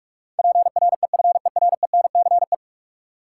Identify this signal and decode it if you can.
Morse code, heard as a keyed tone, about 44 wpm, 700 Hz. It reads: PREFERENCE
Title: Sentences from Top 1500 Words - New Word Drill Alt - 45wpm